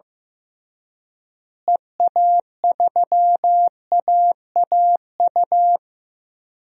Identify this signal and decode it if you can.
Morse code, heard as a keyed tone, about 15 wpm, 700 Hz. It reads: EA3AAU
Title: DXCC-100 Call Signs - 6 Characters - Repeated 3x - 15wpm